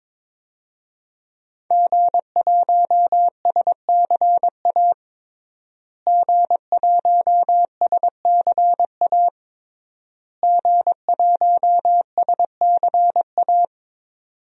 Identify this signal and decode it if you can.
Morse code, heard as a keyed tone, about 22 wpm, 700 Hz. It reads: G1SCA G1SCA G1SCA
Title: DXCC-100 Call Signs - 5 Characters - Repeated 3x - 22wpm